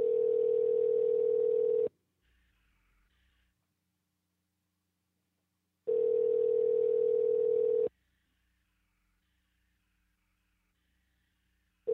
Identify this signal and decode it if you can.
Ringback tone (North America), 440 + 480 Hz.